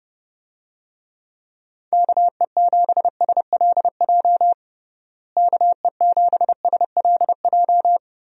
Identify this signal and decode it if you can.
Morse code, heard as a keyed tone, about 32 wpm, 700 Hz. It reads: KE7SLJ KE7SLJ
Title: USA Call Signs - 6 Characters - Repeated 3x - 30wpm